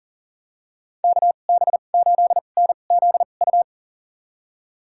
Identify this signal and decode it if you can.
Morse code, heard as a keyed tone, about 40 wpm, spaced as for 29 wpm, 700 Hz. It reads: KB8NZU
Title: USA Call Signs - 6 Characters - 2x Character Spacing - 40wpm